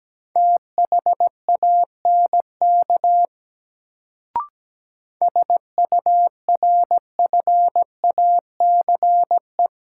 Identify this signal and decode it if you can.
Morse code, heard as a keyed tone, about 17 wpm, 700 Hz. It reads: THANK SURFACE
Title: Sentences from Top 600 Words - New Word Drill - 17wpm